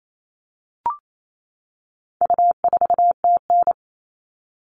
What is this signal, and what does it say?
Morse code, heard as a keyed tone, about 28 wpm, 700 Hz. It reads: U4TN
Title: All Cut Numbers - 4-Digit 28wpm